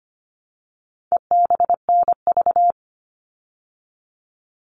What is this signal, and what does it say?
Morse code, heard as a keyed tone, about 25 wpm, 700 Hz. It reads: EBNV
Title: All Cut Numbers - 4-Digit 25wpm